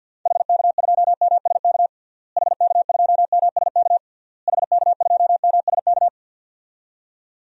Morse code, heard as a keyed tone, about 46 wpm, 700 Hz. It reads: HK2MSK HK2MSK HK2MSK
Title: DXCC-100 Call Signs - 6 Characters - Repeated 3x - 50wpm